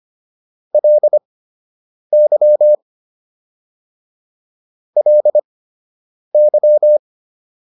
Morse code, heard as a keyed tone, about 25 wpm, spaced as for 8 wpm, 600 Hz. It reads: LY LY